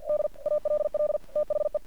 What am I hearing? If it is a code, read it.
Morse code, heard as a keyed tone, about 50 wpm, 600 Hz. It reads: D ALL THE
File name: archived / CRSnet